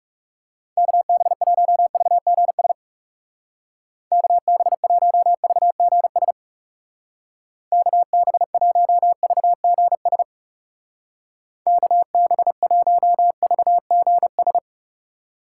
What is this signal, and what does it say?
Morse code, heard as a keyed tone, about 37 wpm, 700 Hz. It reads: KB1VGS KB1VGS KB1VGS KB1VGS
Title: Speed Racing - USA 2x3 Call Signs 30wpm